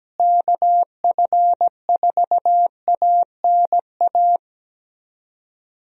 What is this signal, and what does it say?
Morse code, heard as a keyed tone, about 17 wpm, 700 Hz. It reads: KF4ANA
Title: USA Call Signs - 6 Characters - Repeated 3x - 17wpm